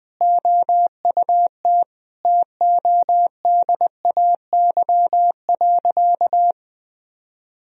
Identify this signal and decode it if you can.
Morse code, heard as a keyed tone, about 20 wpm, 700 Hz. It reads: OUT TODAY.
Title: Sentences from Top 1000 Words - Review Part 2 - 20wpm